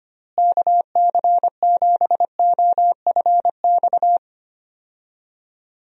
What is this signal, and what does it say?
Morse code, heard as a keyed tone, about 25 wpm, 700 Hz. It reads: KC7OFX
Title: USA Call Signs - 6 Characters - Repeated 3x - 25wpm